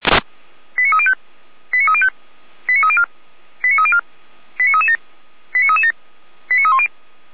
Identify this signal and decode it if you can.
Five-tone selective call (ZVEI), heard as quick runs of five tones, tones of about 70 ms. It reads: 89305 89305 89304 89304 89337 89337 89310